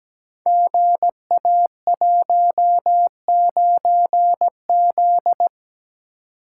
Morse code, heard as a keyed tone, about 17 wpm, 700 Hz. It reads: GA19Z
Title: Code Groups - 5 Letter:Numbers - 17wpm